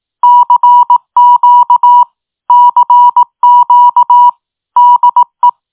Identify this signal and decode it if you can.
Morse code, heard as a keyed tone, about 18 wpm, 970 Hz. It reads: CQ CQ DE